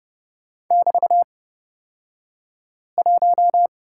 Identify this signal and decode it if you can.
Morse code, heard as a keyed tone, about 30 wpm, spaced as for 5 wpm, 700 Hz. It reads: <BT>1
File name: Lesson 36d - BT - Two Characters 30wpm